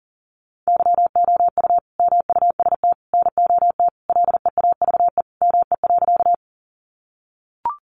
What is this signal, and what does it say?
Morse code, heard as a keyed tone, about 40 wpm, 700 Hz. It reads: YOU MUST NOT LEAVE ME.
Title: Sentences from Top 600 Words 40wpm